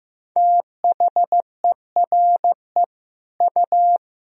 Morse code, heard as a keyed tone, about 15 wpm, 700 Hz. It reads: THERE U
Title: Sentences from Top 100 Words and Abbreviations 15wpm